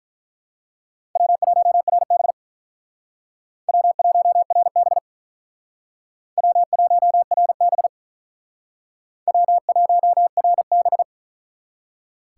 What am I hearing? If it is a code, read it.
Morse code, heard as a keyed tone, about 43 wpm, 700 Hz. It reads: W1RB W1RB W1RB W1RB
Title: Speed Racing - USA 1x2 Call Signs 35wpm